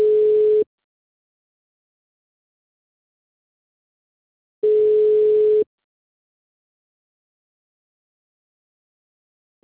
Ringback tone (Europe, 425 Hz).